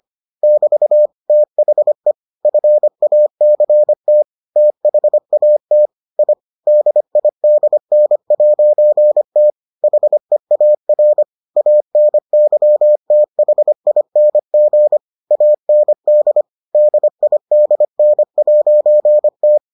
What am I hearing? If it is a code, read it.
Morse code, heard as a keyed tone, about 25 wpm, 600 Hz. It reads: <BT> THE FACT THAT I DIDN'T HEAR ANYTHING AND DIDN'T